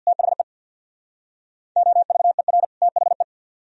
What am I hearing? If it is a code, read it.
Morse code, heard as a keyed tone, about 63 wpm, 700 Hz. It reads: THE OVER THE